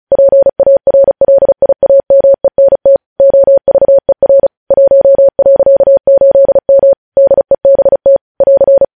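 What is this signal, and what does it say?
Morse code, heard as a keyed tone, about 36 wpm, 550 Hz. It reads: PARLIAMENT OVER 1.8M DEBT <AR>
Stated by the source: Morse code practice sending news headlines at 35 wpm.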